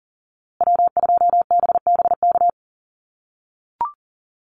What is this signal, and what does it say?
Morse code, heard as a keyed tone, about 40 wpm, 700 Hz. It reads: W2BBK
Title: USA Call Signs Repeated 2x 40wpm